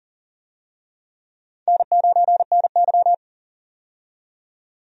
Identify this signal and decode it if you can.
Morse code, heard as a keyed tone, about 40 wpm, 700 Hz. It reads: N9NY
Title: USA Call Signs - 4 Characters - 40wpm